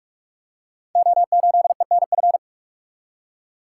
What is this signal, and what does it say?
Morse code, heard as a keyed tone, about 44 wpm, 700 Hz. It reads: O8ENF
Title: Code Groups - 5 Letter:Numbers - 45wpm